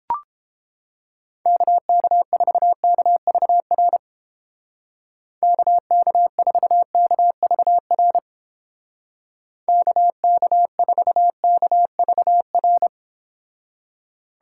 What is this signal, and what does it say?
Morse code, heard as a keyed tone, about 31 wpm, 700 Hz. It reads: KK4KVR KK4KVR KK4KVR